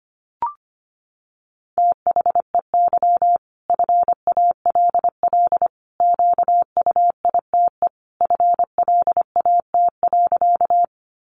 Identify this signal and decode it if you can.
Morse code, heard as a keyed tone, about 25 wpm, 700 Hz. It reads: THEY FALL QUITE FLAT.
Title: Sentences from Top 1000 Words 25wpm